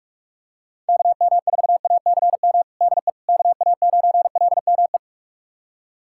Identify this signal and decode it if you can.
Morse code, heard as a keyed tone, about 46 wpm, 700 Hz. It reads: KM4ACM DE KA9LME
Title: USA Callsign DE Callsign 45wpm